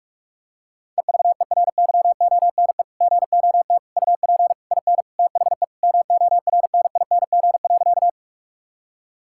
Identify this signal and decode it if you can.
Morse code, heard as a keyed tone, about 41 wpm, 700 Hz. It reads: EVERYONE GOT UP IN THE MORNING.
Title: Sentences from Top 1000 Words - Review Part 5 - 45wpm